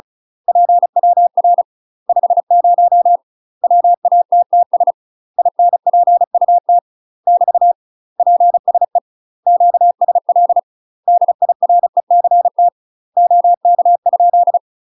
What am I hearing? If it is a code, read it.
Morse code, heard as a keyed tone, about 35 wpm, 700 Hz. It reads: PWR 50 WATTS INPUT <BT> PSE QSL DIRECT OK?
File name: UK MASTER QSOs TRAINING / 35WPM / UK MASTER QSOs 0008